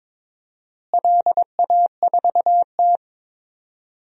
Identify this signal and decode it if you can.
Morse code, heard as a keyed tone, about 22 wpm, 700 Hz. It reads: LA4T
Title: DXCC-100 Call Signs - 4 Characters - Repeated 3x - 22wpm